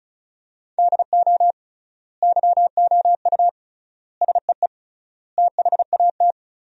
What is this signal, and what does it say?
Morse code, heard as a keyed tone, about 36 wpm, 700 Hz. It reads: DO YOU SEE THAT
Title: 3x Word-Spacing - Sentences from Top 500 Words - 35wpm